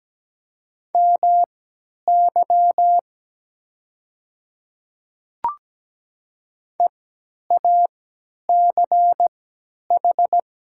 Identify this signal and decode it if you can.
Morse code, heard as a keyed tone, about 17 wpm, spaced as for 10 wpm, 690 Hz. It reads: MY EACH
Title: 3x Character Spacing - Top 100 Words - 17wpm